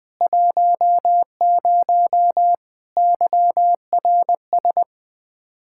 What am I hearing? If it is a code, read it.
Morse code, heard as a keyed tone, about 20 wpm, 700 Hz. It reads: 10 YRS
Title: Practice QSO - Exchange 3 - More Info - 20wpm